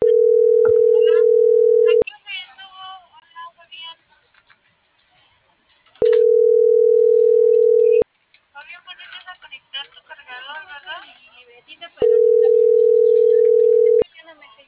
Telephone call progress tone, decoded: ringback tone

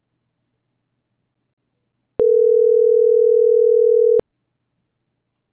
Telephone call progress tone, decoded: ringback tone

